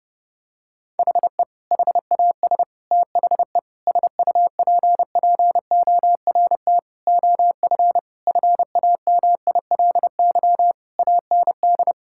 Morse code, heard as a keyed tone, about 30 wpm, 700 Hz. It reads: HE HAS THE SUPPORT OF FAMILY AND
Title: Sentences from Top 1000 Words - Review Part 5 - 30wpm